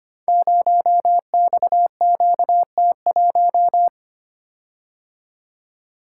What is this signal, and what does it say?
Morse code, heard as a keyed tone, about 25 wpm, 700 Hz. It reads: 0XQT1
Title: Code Groups - 5 Letter:Numbers - 25wpm